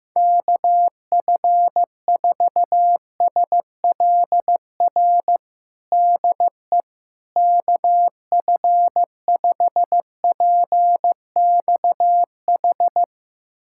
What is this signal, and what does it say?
Morse code, heard as a keyed tone, about 15 wpm, 700 Hz. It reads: KF4SLR DE KF5PXH
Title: USA Callsign DE Callsign - 6 Characters - 15wpm